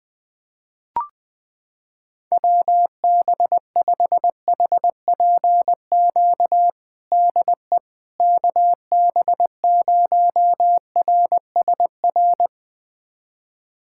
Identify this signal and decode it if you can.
Morse code, heard as a keyed tone, about 20 wpm, 700 Hz. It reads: WB5HPQ DE KB0RSR
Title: USA Callsign DE Callsign - 6 Characters - 20wpm